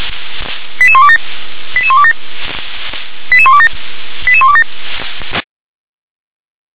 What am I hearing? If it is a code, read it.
Five-tone selective call (ZVEI), heard as quick runs of five tones, tones of about 70 ms. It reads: 88127 88127 88127 88127
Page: /357/SDR/ZVEI/88040/2021.10.01/